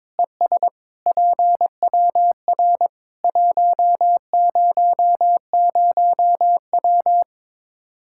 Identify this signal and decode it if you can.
Morse code, heard as a keyed tone, about 21 wpm, 700 Hz. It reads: ES PWR 100W
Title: Practice QSO - Exchange 2 - Info - 22wpm